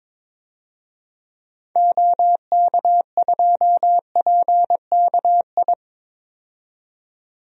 Morse code, heard as a keyed tone, about 22 wpm, 700 Hz. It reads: OK2PKI
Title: DXCC-100 Call Signs - 6 Characters - 22wpm